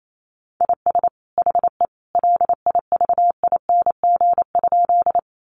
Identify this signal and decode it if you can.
Morse code, heard as a keyed tone, about 28 wpm, 700 Hz. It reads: IS HE LIVING?
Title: Sentences from Top 1100 Words 28wpm